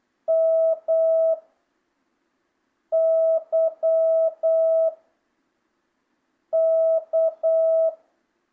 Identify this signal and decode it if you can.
Morse code, heard as a keyed tone, about 8 wpm, 1300 Hz. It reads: M Y K
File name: jh8cbh / 2025cw